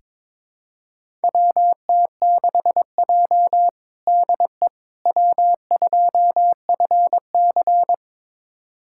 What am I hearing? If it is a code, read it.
Morse code, heard as a keyed tone, about 22 wpm, 700 Hz. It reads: WT6J DE W2FC
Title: USA Callsign DE Callsign - 4 Characters - 22wpm